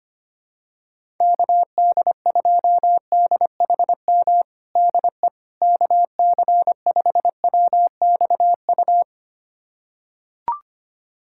Morse code, heard as a keyed tone, about 25 wpm, 700 Hz. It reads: KD2DHM DE KC5WXU
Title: USA Callsign DE Callsign - 6 Characters - 25wpm